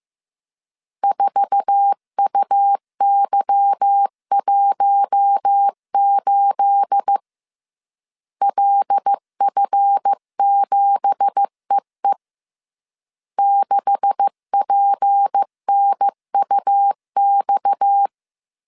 Morse code, heard as a keyed tone, about 15 wpm, 800 Hz. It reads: 4UY18 LF7EE 6PNUX
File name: CW / DET / 1 9 1500C S A 14wpm 3222